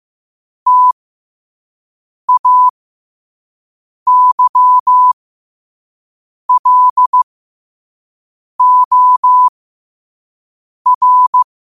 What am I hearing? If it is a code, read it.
Morse code, heard as a keyed tone, about 15 wpm, spaced as for 5 wpm, 1000 Hz. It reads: TAYLOR
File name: qotdfiles / 5wpm